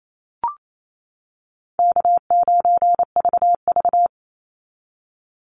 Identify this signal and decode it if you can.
Morse code, heard as a keyed tone, about 28 wpm, 700 Hz. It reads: K9VV
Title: Call Signs - 28wpm